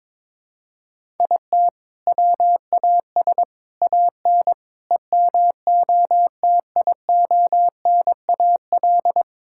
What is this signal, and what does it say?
Morse code, heard as a keyed tone, about 22 wpm, 700 Hz. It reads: IT WAS AN EMOTIONAL
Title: Sentences from Top 1000 Words - Review Part 5 - 22wpm